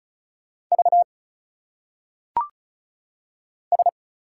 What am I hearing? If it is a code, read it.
Morse code, heard as a keyed tone, about 34 wpm, 700 Hz. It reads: V S
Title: Single Letters - Phonetic Pronunciation 35wpm